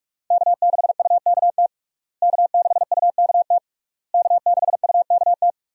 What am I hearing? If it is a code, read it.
Morse code, heard as a keyed tone, about 41 wpm, 700 Hz. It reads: K6UKT K6UKT K6UKT
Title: USA Call Signs - 5 Characters - Repeated 3x - 45wpm